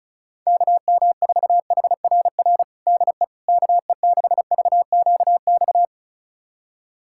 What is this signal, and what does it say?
Morse code, heard as a keyed tone, about 36 wpm, 700 Hz. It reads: KM4HRR DE KE6VQX
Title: USA Callsign DE Callsign 35wpm